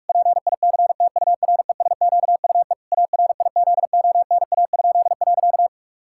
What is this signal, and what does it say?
Morse code, heard as a keyed tone, about 41 wpm, 700 Hz. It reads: PICTURESQUE ARIZONA?.
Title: Pangram Challenge v2 - 45wpm